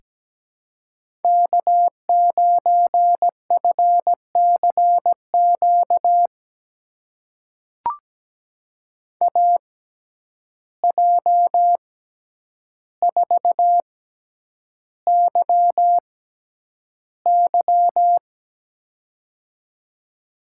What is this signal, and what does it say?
Morse code, heard as a keyed tone, about 17 wpm, spaced as for 9 wpm, 700 Hz. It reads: K9FCQ AJ4YY
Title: USA Call Signs - 5 Characters - 6x Character Spacing - 17wpm